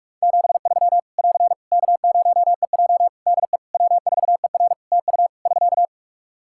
Morse code, heard as a keyed tone, about 44 wpm, 700 Hz. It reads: 73 <AR> K0EJ DE W4ER TU <SK>
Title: Practice QSO - Exchange 4 - Ending - 45wpm